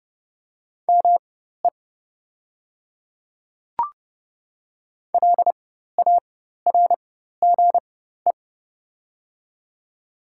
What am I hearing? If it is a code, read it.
Morse code, heard as a keyed tone, about 29 wpm, spaced as for 14 wpm, 700 Hz. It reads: ME LARGE
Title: 4x Character Spacing - Top 200 Words - 30wpm